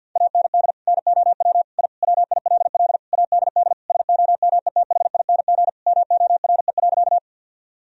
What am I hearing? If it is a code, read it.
Morse code, heard as a keyed tone, about 51 wpm, 700 Hz. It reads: AND NOW I WILL ADD SOMETHING MORE.
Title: Sentences from Top 1000 Words - Review Part 1 - 50wpm